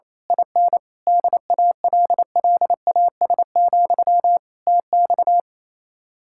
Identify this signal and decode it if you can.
Morse code, heard as a keyed tone, about 26 wpm, 700 Hz. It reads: IN DALLAS, TX
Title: Speed-Racing Common QSO Phrases - 28wpm